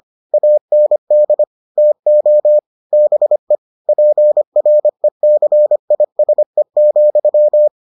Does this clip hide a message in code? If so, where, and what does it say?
Yes: Morse code from 0.3 to 7.7 s, heard as a keyed tone, about 25 wpm, 600 Hz. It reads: AND TO BE PRECISE,